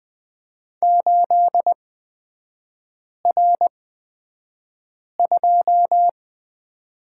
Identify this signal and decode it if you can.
Morse code, heard as a keyed tone, about 20 wpm, spaced as for 5 wpm, 700 Hz. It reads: 8R2
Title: 236 Lesson 38e - Letter q - Three Characters 20wpm